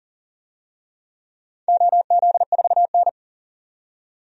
Morse code, heard as a keyed tone, about 37 wpm, 700 Hz. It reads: OZ4N
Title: DXCC-100 Call Signs - 4 Characters - 40wpm